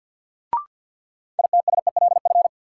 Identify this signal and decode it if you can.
Morse code, heard as a keyed tone, about 47 wpm, 700 Hz. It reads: ITSELF
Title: Sentences from Top 800 Words - New Word Drill - 50wpm